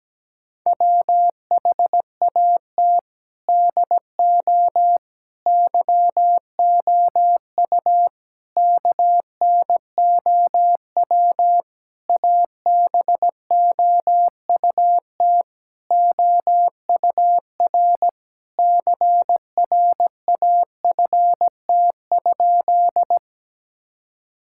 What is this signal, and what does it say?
Morse code, heard as a keyed tone, about 17 wpm, 700 Hz. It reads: WHAT DO YOU KNOW ABOUT OUR CRAFT?
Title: Sentences from Top 1000 Words - Review Part 6 - 17wpm